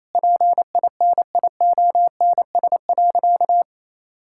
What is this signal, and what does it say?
Morse code, heard as a keyed tone, about 28 wpm, 700 Hz. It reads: PINIONS.
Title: Pangram Challenge v2 - 28wpm